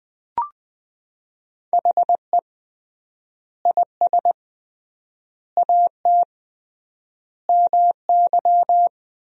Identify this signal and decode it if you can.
Morse code, heard as a keyed tone, about 20 wpm, 700 Hz. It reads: HE IS AT MY
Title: 3x Word-Spacing - Sentences from Top 200 Words - 20wpm